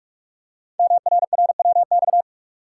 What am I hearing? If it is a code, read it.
Morse code, heard as a keyed tone, about 44 wpm, 700 Hz. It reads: MRRWX
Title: Code Groups - 5 Letters - 45wpm